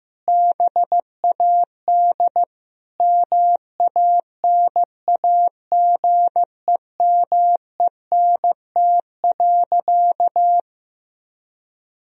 Morse code, heard as a keyed tone, about 15 wpm, 700 Hz. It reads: BAD MANAGEMENT.
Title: Sentences from Top 1000 Words - Review Part 4 - 15wpm